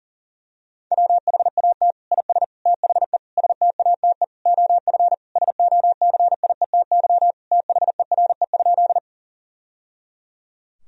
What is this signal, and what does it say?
Morse code, heard as a keyed tone, about 40 wpm, 700 Hz. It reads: WHAT IS THE STATE OF SOCIETY THERE?
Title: Sentences from Top 1000 Words - Review Part 4 - 40wpm